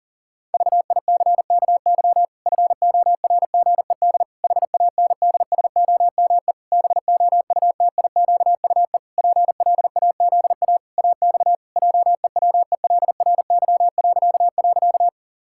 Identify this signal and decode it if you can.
Morse code, heard as a keyed tone, about 40 wpm, 700 Hz. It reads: VICKY FORGED HANDSOME BOUTIQUE PLAZA AX JEWELRY..
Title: Pangram Challenge v2 - 40wpm